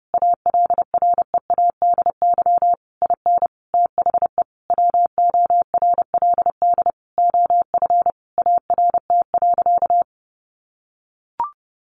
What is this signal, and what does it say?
Morse code, heard as a keyed tone, about 30 wpm, 700 Hz. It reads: ALREADY IN THE WORLD OF ART.
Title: Sentences from Top 1500 Words 30wpm